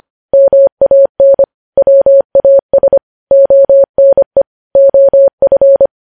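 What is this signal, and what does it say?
Morse code, heard as a keyed tone, about 25 wpm, 550 Hz. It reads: MAN WAS ONE OF
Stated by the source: News headlines sent in Morse code at 25 wpm.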